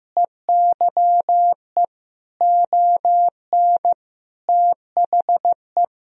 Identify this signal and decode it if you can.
Morse code, heard as a keyed tone, about 15 wpm, 700 Hz. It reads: EYE ON THE
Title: Sentences from Top 300 Words and Abbreviations 15wpm